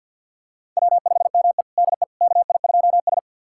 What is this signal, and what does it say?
Morse code, heard as a keyed tone, about 52 wpm, 700 Hz. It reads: W5ME DE KI2S
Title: USA Callsign DE Callsign - 4 Characters - 50wpm